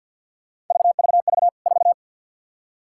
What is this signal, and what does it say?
Morse code, heard as a keyed tone, about 53 wpm, 700 Hz. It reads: VVV 4
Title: K1USN SST QSO Practice 50wpm